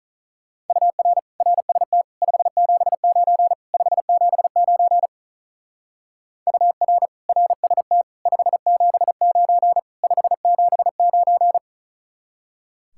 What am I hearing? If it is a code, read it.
Morse code, heard as a keyed tone, about 45 wpm, 700 Hz. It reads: UR RST 579 579 UR RST 579 579